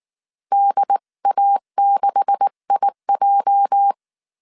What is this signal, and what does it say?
Morse code, heard as a keyed tone, about 19 wpm, 800 Hz. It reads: DA6IJ
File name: CW / DET / 1 9 1500C S B 18wpm 2551